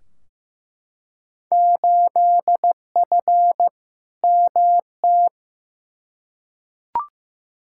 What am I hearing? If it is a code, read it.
Morse code, heard as a keyed tone, about 15 wpm, 700 Hz. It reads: 8F MT
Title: Field Day Exchanges Encore - 15wpm